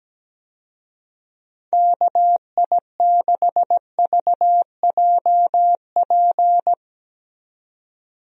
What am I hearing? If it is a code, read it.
Morse code, heard as a keyed tone, about 17 wpm, 700 Hz. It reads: KI6VJP
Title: USA Call Signs - 6 Characters - 17wpm